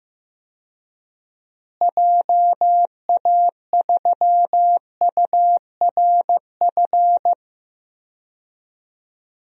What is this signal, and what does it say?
Morse code, heard as a keyed tone, about 15 wpm, 700 Hz. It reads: JA3URF
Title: DXCC-100 Call Signs - 6 Characters - Repeated 2x - 15wpm